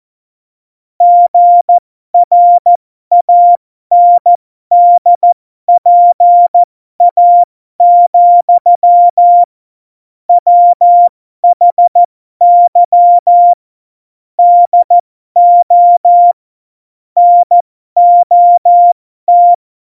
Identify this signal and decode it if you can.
Morse code, heard as a keyed tone, about 14 wpm, spaced as for 12 wpm, 700 Hz. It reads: GRANDPA, WHY DO NOT
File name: Morsekurs / Witze